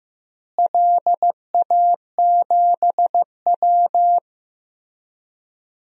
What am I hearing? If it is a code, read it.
Morse code, heard as a keyed tone, about 15 wpm, 700 Hz. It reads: LA7W
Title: DXCC-100 Call Signs - 4 Characters - Repeated 3x - 15wpm